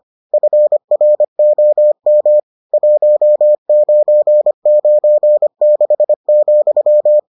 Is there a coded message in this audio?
Yes: Morse code from 0.3 to 7.2 s, heard as a keyed tone, about 25 wpm, 600 Hz. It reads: FROM 1996,